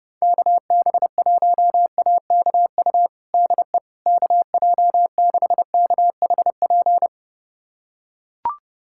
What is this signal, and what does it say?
Morse code, heard as a keyed tone, about 30 wpm, 700 Hz. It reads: KB1AKU DE KJ6KHP
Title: USA Callsign DE Callsign - 6 Characters - 30wpm